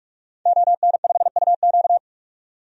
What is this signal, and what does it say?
Morse code, heard as a keyed tone, about 46 wpm, 700 Hz. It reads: ON5UQ
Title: DXCC-100 Call Signs - 5 Characters - 45wpm